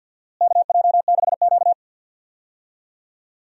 Morse code, heard as a keyed tone, about 50 wpm, 700 Hz. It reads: KJ6Q
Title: USA Call Signs - 4 Characters - Repeated 3x - 50wpm